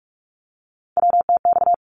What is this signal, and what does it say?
Morse code, heard as a keyed tone, about 48 wpm, 700 Hz. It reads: WTX
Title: ARRL Sections 45wpm